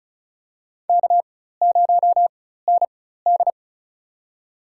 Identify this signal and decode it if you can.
Morse code, heard as a keyed tone, about 35 wpm, spaced as for 16 wpm, 700 Hz. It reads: K0ND
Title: USA Call Signs - 4 Characters - 4x Character Spacing - 35wpm